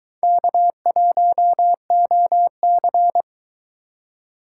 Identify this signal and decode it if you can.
Morse code, heard as a keyed tone, about 22 wpm, 700 Hz. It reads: K1OC